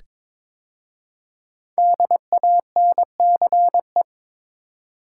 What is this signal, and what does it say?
Morse code, heard as a keyed tone, about 22 wpm, 700 Hz. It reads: DANCE
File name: 5 Letter Words - First 3 Letters Given - 22wpm